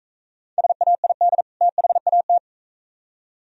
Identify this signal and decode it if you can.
Morse code, heard as a keyed tone, about 42 wpm, 700 Hz. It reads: SAID THAT